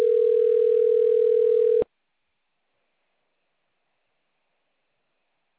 Ringback tone (North America), 440 + 480 Hz.